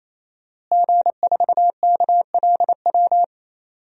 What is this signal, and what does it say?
Morse code, heard as a keyed tone, about 27 wpm, 700 Hz. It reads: G4KLW
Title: DXCC-100 Call Signs - 5 Characters - Repeated 3x - 28wpm